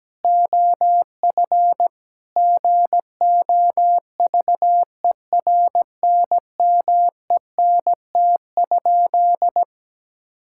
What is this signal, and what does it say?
Morse code, heard as a keyed tone, about 17 wpm, 700 Hz. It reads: OF GOVERNMENT?
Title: Sentences from Top 1000 Words - Review Part 5 - 17wpm